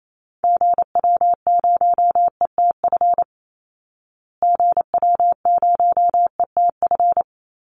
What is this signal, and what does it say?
Morse code, heard as a keyed tone, about 28 wpm, 700 Hz. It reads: GW0ETF GW0ETF
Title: Call Signs Repeated 3x - 28wpm